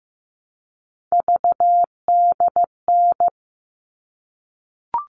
Morse code, heard as a keyed tone, about 15 wpm, 700 Hz. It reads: VDN
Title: All Cut Numbers - 3-Digit 15wpm